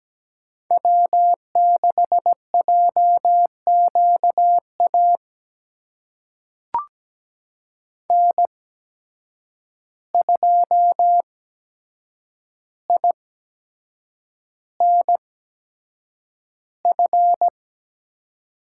Morse code, heard as a keyed tone, about 17 wpm, spaced as for 7 wpm, 700 Hz. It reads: W6JQA N2INF
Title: USA Call Signs - 5 Characters - 8x Character Spacing - 17wpm